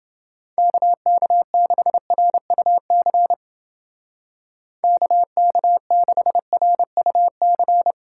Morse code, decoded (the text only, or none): KK6RUC KK6RUC